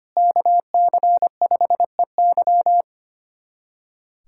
Morse code, heard as a keyed tone, about 25 wpm, 700 Hz. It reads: KC5EY
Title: USA Callsign DE Callsign 25wpm